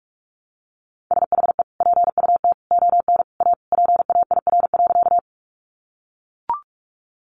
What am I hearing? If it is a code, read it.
Morse code, heard as a keyed tone, about 44 wpm, 700 Hz. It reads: SHE PUT ON A PAIR.
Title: Sentences from Top 1200 Words 45wpm